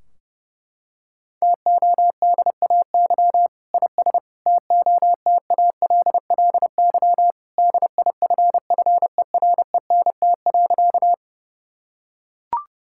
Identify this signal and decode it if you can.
Morse code, heard as a keyed tone, about 32 wpm, 700 Hz. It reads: TODAY IS TOTALLY DIFFERENT.
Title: Sentences from Top 1600 Words 30wpm